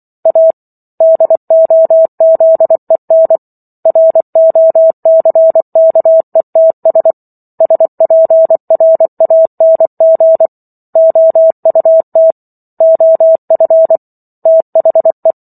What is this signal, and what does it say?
Morse code, heard as a keyed tone, about 24 wpm, 650 Hz. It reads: A DOZEN ROCKETS SPRANG OUT OF THE